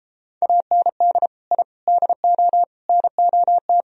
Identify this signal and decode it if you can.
Morse code, heard as a keyed tone, about 34 wpm, 700 Hz. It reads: AND I DO NOT